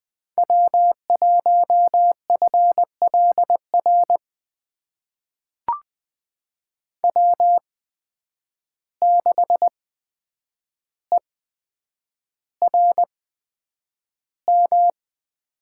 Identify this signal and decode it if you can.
Morse code, heard as a keyed tone, about 20 wpm, spaced as for 9 wpm, 700 Hz. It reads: W1FLR W6ERM
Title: USA Call Signs - 5 Characters - 8x Character Spacing - 20wpm